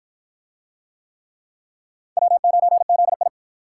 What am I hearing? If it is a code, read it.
Morse code, heard as a keyed tone, about 53 wpm, 700 Hz. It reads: W9ZI